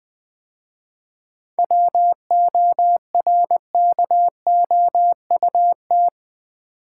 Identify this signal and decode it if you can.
Morse code, heard as a keyed tone, about 20 wpm, 700 Hz. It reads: WORKOUT
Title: 7 Letter Words - First 3 Letters Given - 20wpm